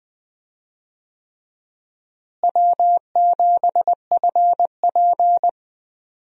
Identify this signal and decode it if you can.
Morse code, heard as a keyed tone, about 20 wpm, 700 Hz. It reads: W7FP
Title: USA Call Signs - 4 Characters - Repeated 3x - 20wpm